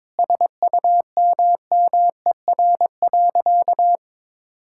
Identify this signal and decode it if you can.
Morse code, heard as a keyed tone, about 22 wpm, 700 Hz. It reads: SUMMER.
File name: practice / 22wpm / Sentences with 8-Word Phrases - 22wpm